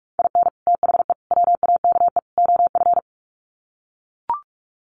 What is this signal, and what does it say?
Morse code, heard as a keyed tone, about 44 wpm, 700 Hz. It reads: IN THE WAKE OF
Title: Sets of 4 Words 45wpm